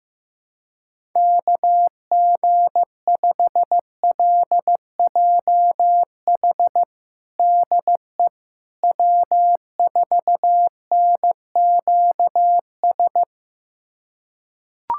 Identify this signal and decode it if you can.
Morse code, heard as a keyed tone, about 15 wpm, 700 Hz. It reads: KG5LJH DE W4NQS
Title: USA Callsign DE Callsign 15wpm